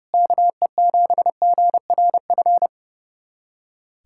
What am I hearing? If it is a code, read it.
Morse code, heard as a keyed tone, about 32 wpm, 700 Hz. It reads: KE7GRF
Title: USA Callsign DE Callsign 30wpm